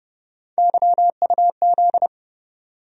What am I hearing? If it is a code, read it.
Morse code, heard as a keyed tone, about 30 wpm, 700 Hz. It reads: YUZ
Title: USA Call Sign Postfixes - 3 Letters 30wpm